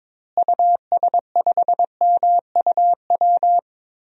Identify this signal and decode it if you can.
Morse code, heard as a keyed tone, about 22 wpm, 700 Hz. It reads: US5MUW
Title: DXCC-100 Call Signs - 6 Characters - Repeated 3x - 22wpm